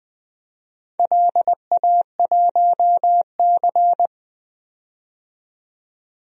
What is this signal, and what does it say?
Morse code, heard as a keyed tone, about 20 wpm, 700 Hz. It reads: LA1C
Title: DXCC-100 Call Signs - 4 Characters - 20wpm